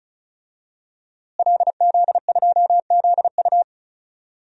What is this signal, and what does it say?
Morse code, heard as a keyed tone, about 34 wpm, 700 Hz. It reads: LZ2ZU
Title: DXCC-100 Call Signs - 5 Characters - Repeated 2x - 35wpm